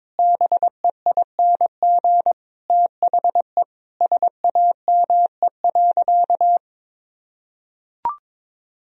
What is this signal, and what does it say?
Morse code, heard as a keyed tone, about 22 wpm, 700 Hz. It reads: BEING THE SAME.
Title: Sentences from Top 1000 Words - Review Part 3 - 22wpm